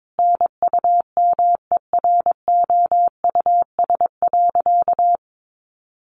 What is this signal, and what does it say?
Morse code, heard as a keyed tone, about 22 wpm, 700 Hz. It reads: NUMEROUS.
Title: Sentences from Top 1400 Words 22wpm